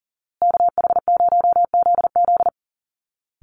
Morse code, heard as a keyed tone, about 40 wpm, 700 Hz. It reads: KH0ZZ
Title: Call Signs Repeated 2x - 40wpm